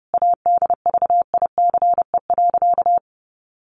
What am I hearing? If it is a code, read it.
Morse code, heard as a keyed tone, about 30 wpm, 700 Hz. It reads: ADVICE.
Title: Sentences from Top 1100 Words 30wpm